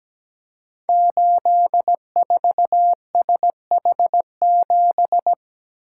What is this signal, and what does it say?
Morse code, heard as a keyed tone, about 17 wpm, 700 Hz. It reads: 84SH7
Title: Code Groups - 5 Letter:Numbers - 17wpm